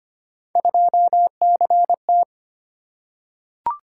Morse code, heard as a keyed tone, about 25 wpm, 700 Hz. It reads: 2CT
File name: Three Letter-Number Combinations - 25wpm